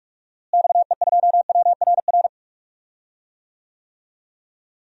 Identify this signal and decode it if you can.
Morse code, heard as a keyed tone, about 42 wpm, 700 Hz. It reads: XE2WRR
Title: DXCC-100 Call Signs - 6 Characters - Repeated 2x - 45wpm